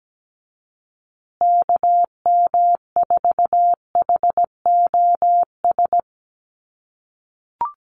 Morse code, heard as a keyed tone, about 17 wpm, 700 Hz. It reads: KM4HOS
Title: USA Call Signs Repeated 3x 17wpm